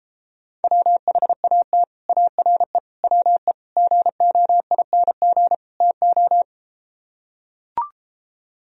Morse code, heard as a keyed tone, about 33 wpm, 700 Hz. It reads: WHAT ARE WE GOING TO